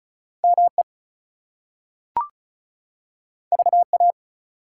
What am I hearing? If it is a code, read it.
Morse code, heard as a keyed tone, about 36 wpm, 700 Hz. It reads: ME VA
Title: US States and Canadian Provinces 35wpm